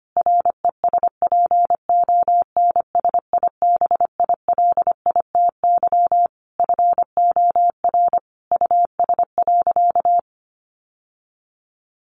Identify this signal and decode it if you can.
Morse code, heard as a keyed tone, about 25 wpm, 700 Hz. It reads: RESPONSIBILITY FOR US.
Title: Sentences from Top 1500 Words 25wpm